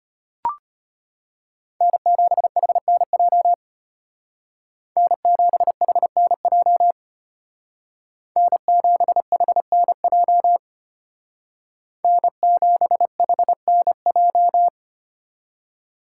Morse code, heard as a keyed tone, about 32 wpm, 700 Hz. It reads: N7HNJ N7HNJ N7HNJ N7HNJ